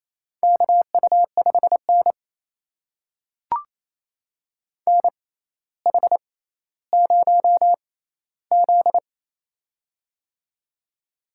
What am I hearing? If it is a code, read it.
Morse code, heard as a keyed tone, about 28 wpm, spaced as for 14 wpm, 700 Hz. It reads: KU5N NH0Z
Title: USA Call Signs - 4 Characters - 6x Character Spacing - 28wpm